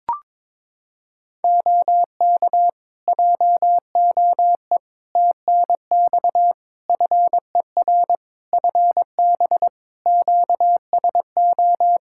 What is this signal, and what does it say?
Morse code, heard as a keyed tone, about 22 wpm, 700 Hz. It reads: OK JOE TNX FER FB QSO
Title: Practice QSO - Exchange 4 - Ending - 22wpm